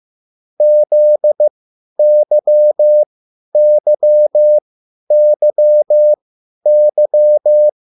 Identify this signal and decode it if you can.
Morse code, heard as a keyed tone, about 15 wpm, spaced as for 10 wpm, 600 Hz. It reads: ZYYYY